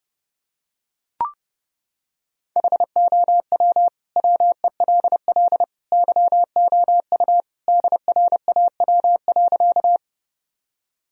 Morse code, heard as a keyed tone, about 30 wpm, 700 Hz. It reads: HOW WELL YOU DRAW.
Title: Sentences from Top 1000 Words - Review Part 1 - 30wpm